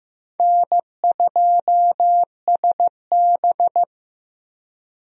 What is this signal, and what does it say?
Morse code, heard as a keyed tone, about 15 wpm, 700 Hz. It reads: N2SB
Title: USA Call Signs - 4 Characters - Repeated 2x - 15wpm